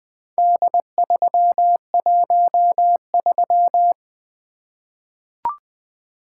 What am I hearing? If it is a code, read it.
Morse code, heard as a keyed tone, about 20 wpm, 700 Hz. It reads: D313
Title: Four Letter-Number Combinations - 20wpm